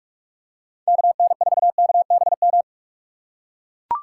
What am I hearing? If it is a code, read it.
Morse code, heard as a keyed tone, about 41 wpm, 700 Hz. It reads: KN4KBM
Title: USA Call Signs - 2x Character Spacing - 45wpm